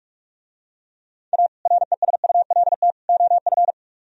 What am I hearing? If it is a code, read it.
Morse code, heard as a keyed tone, about 48 wpm, 700 Hz. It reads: A RESULT OF